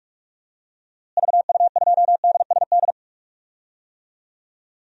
Morse code, heard as a keyed tone, about 47 wpm, 700 Hz. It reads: VU2DSD
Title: DXCC-100 Call Signs - 6 Characters - Repeated 2x - 45wpm